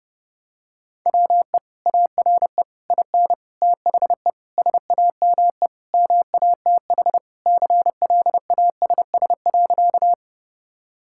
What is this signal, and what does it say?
Morse code, heard as a keyed tone, about 32 wpm, 700 Hz. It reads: WE ARE IN THE SAME MATH CLASS.
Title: Sentences from Top 1000 Words - Review Part 6 - 30wpm